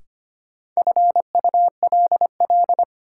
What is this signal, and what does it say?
Morse code, heard as a keyed tone, about 25 wpm, 700 Hz. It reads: FULL
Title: Sentences from Top 500 Words - New Word Drill - 25wpm